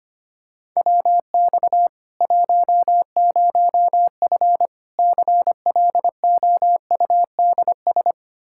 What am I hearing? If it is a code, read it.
Morse code, heard as a keyed tone, about 25 wpm, 700 Hz. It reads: WX 10F CLOUDS
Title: Practice QSO - Exchange 2 - Info - 25wpm